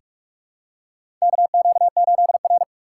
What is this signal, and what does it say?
Morse code, heard as a keyed tone, about 46 wpm, 700 Hz. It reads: KQ8R
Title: USA Call Signs - 4 Characters - 2x Character Spacing - 45wpm